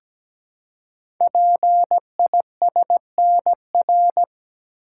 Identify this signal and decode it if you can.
Morse code, heard as a keyed tone, about 17 wpm, 700 Hz. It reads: PISNR
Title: Code Groups - 5 Letters - 17wpm